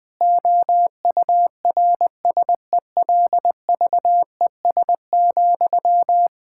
Morse code, heard as a keyed tone, about 19 wpm, 700 Hz. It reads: OURSELVES,
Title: Sentences from Top 1600 Words 20wpm